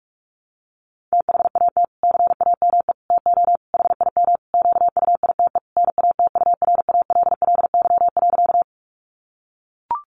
Morse code, heard as a keyed tone, about 44 wpm, 700 Hz. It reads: THAT CAME TO HIM QUITE NATURALLY.
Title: Sentences from Top 1300 Words 45wpm